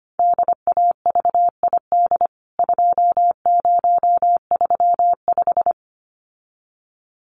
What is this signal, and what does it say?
Morse code, heard as a keyed tone, about 25 wpm, 700 Hz. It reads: DAVID 2035
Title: CWT Exchange 25wpm